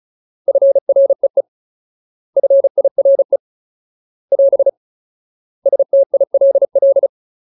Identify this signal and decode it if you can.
Morse code, heard as a keyed tone, about 35 wpm, 550 Hz. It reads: FREE FIRE <AS> STILL
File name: Ultimate Word List / 35WPM Ultimate